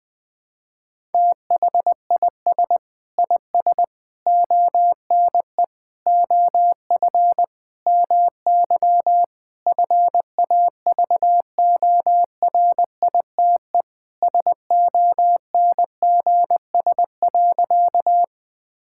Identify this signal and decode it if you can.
Morse code, heard as a keyed tone, about 20 wpm, 700 Hz. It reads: THIS IS ONE OF MY FAVORITE SONGS.
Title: Sentences with 4-Word Phrases - Spoken Beginning - 20wpm